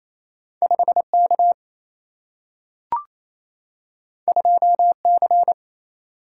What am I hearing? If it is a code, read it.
Morse code, heard as a keyed tone, about 28 wpm, 700 Hz. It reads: 5K 2C
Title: Single-Letter Single-Number 28wpm